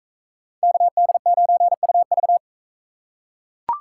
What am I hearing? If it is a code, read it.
Morse code, heard as a keyed tone, about 42 wpm, 700 Hz. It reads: KD9UV